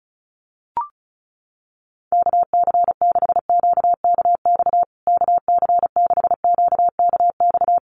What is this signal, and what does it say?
Morse code, heard as a keyed tone, about 34 wpm, 700 Hz. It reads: KC6QKX KC6QKX
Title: USA Call Signs Repeated 3x 35wpm